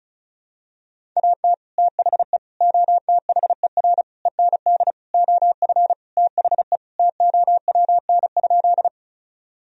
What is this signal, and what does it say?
Morse code, heard as a keyed tone, about 34 wpm, 700 Hz. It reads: AT THE OTHER END OF THE TOWN?
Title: Sentences from Top 1000 Words - Review Part 2 - 35wpm